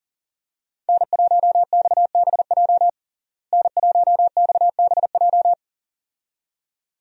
Morse code, heard as a keyed tone, about 37 wpm, 700 Hz. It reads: N1XBJ N1XBJ
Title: USA Call Signs - 5 Characters - Repeated 2x - 40wpm